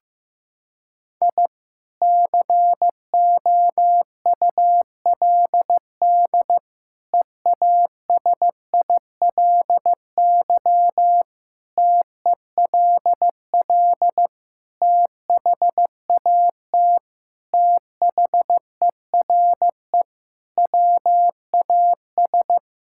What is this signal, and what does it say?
Morse code, heard as a keyed tone, about 15 wpm, 700 Hz. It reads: I COULD EASILY TELL THAT THERE WAS
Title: Sentences from Top 1000 Words - Review Part 6 - 15wpm